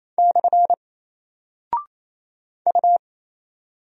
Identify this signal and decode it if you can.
Morse code, heard as a keyed tone, about 27 wpm, 700 Hz. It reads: / U
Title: Single Character (Letter:Number:Punctuation:Prosign) Alt 28wpm